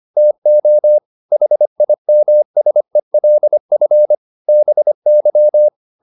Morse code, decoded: TO HIMSELF BY

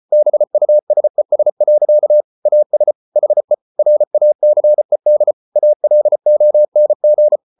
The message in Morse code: BUSES. AS HE RACED ALONG

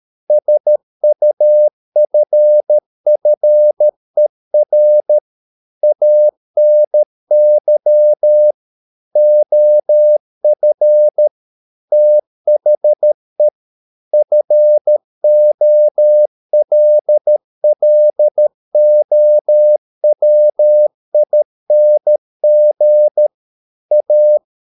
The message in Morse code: SUFFER ANY OF THE FOLLOWING A

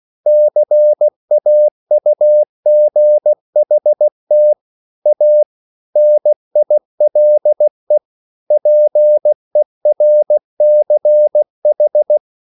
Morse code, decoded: CAUGHT A NILE PERCH

